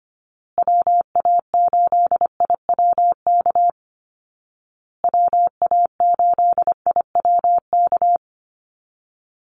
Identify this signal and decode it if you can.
Morse code, heard as a keyed tone, about 25 wpm, 700 Hz. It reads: WA8IWK WA8IWK
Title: Call Signs Repeated 2x - 25wpm